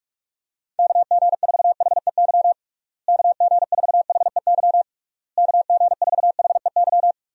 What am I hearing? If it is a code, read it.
Morse code, heard as a keyed tone, about 46 wpm, 700 Hz. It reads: KG4HEY KG4HEY KG4HEY
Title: USA Call Signs - 6 Characters - Repeated 3x - 45wpm